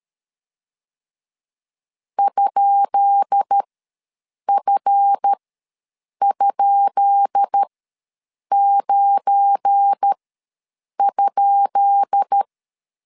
Morse code, heard as a keyed tone, about 13 wpm, spaced as for 7 wpm, 800 Hz. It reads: ?F?9?
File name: LEZIONI / 10 SIGNS / 10B INT ws 12wpm 3527